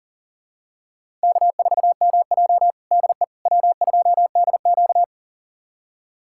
Morse code, heard as a keyed tone, about 40 wpm, 700 Hz. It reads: K4MJ DE W2DQ
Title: USA Callsign DE Callsign - 4 Characters - 40wpm